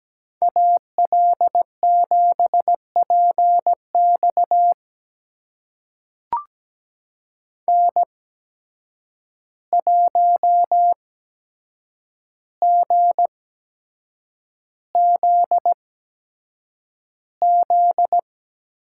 Morse code, heard as a keyed tone, about 17 wpm, spaced as for 7 wpm, 700 Hz. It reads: AL7PX N1GZZ
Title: USA Call Signs - 5 Characters - 8x Character Spacing - 17wpm